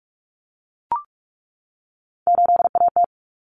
Morse code, heard as a keyed tone, about 44 wpm, 700 Hz. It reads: 8AT
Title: Common Cut Numbers - 3-Digit 45wpm